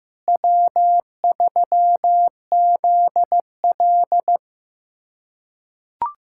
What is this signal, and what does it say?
Morse code, heard as a keyed tone, about 15 wpm, 700 Hz. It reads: W3ZL
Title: USA Callsign DE Callsign - 4 Characters - 15wpm